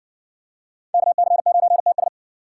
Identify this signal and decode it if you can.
Morse code, heard as a keyed tone, about 54 wpm, 700 Hz. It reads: KC9TS